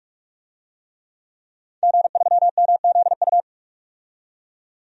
Morse code, heard as a keyed tone, about 46 wpm, 700 Hz. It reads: G3MZU
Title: DXCC-100 Call Signs - 5 Characters - 45wpm